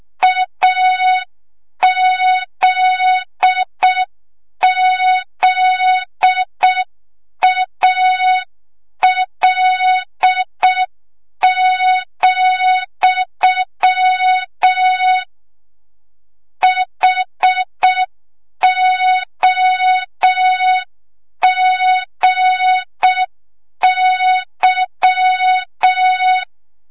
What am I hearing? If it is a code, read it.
Morse code, heard as a keyed tone, about 6 wpm, 3700 Hz. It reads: AZZAL, HOGY
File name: Morze-HA2MN / hang / 25 GYAKORLAT